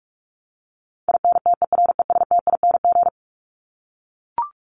Morse code, heard as a keyed tone, about 44 wpm, 700 Hz. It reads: INTERESTING
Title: ING Suffix - Top 100 Words with Base Words 45wpm